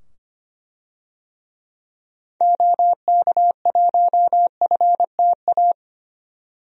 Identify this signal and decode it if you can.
Morse code, heard as a keyed tone, about 25 wpm, 700 Hz. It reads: OK1FTA
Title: DXCC-100 Call Signs - 6 Characters - 25wpm